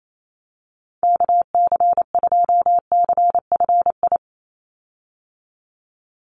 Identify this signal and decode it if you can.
Morse code, heard as a keyed tone, about 28 wpm, 700 Hz. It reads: KC2CFI
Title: USA Call Signs 28wpm